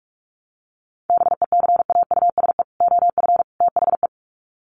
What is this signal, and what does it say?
Morse code, heard as a keyed tone, about 44 wpm, 700 Hz. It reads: BECAUSE OF THE
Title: Sets of 3 Words 45wpm